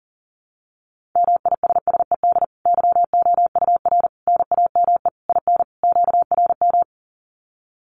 Morse code, heard as a keyed tone, about 37 wpm, 700 Hz. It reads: MISSED YOUR NAME IN QRM
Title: Common QSO Phrases 40wpm